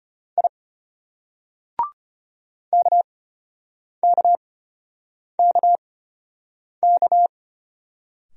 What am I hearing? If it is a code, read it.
Morse code, heard as a keyed tone, about 34 wpm, spaced as for 8 wpm, 700 Hz. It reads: I KKKK